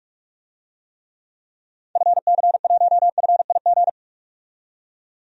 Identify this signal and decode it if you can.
Morse code, heard as a keyed tone, about 45 wpm, 700 Hz. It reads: FC1FIG
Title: DXCC-100 Call Signs - 6 Characters - 45wpm